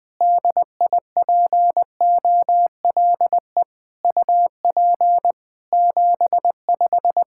Morse code, heard as a keyed tone, about 20 wpm, 700 Hz. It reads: DIPOLE UP 75
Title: Practice QSO - Exchange 2 - Info - 20wpm